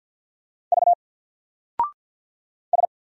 Morse code, heard as a keyed tone, about 50 wpm, 700 Hz. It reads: V S
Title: Single Character (Letter:Number:Punctuation:Prosign) 50wpm